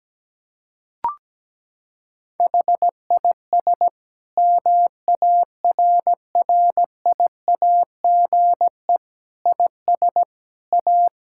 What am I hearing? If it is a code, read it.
Morse code, heard as a keyed tone, about 17 wpm, 700 Hz. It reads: HIS MARRIAGE IS A
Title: Sentences from Top 1000 Words - Review Part 6 - 17wpm